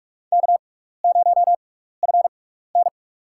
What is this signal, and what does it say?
Morse code, heard as a keyed tone, about 46 wpm, spaced as for 16 wpm, 700 Hz. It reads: K0FN
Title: USA Call Signs - 4 Characters - 6x Character Spacing - 45wpm